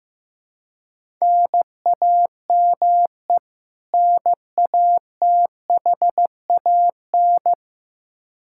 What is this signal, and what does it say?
Morse code, heard as a keyed tone, about 15 wpm, 700 Hz. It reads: NAME NATHAN
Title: Practice QSO - Exchange 1 - Beginning - 15wpm